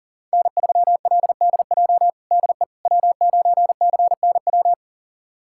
Morse code, heard as a keyed tone, about 40 wpm, 700 Hz. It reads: N3LDJ DE W9CNW
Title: USA Callsign DE Callsign - 5 Characters - 40wpm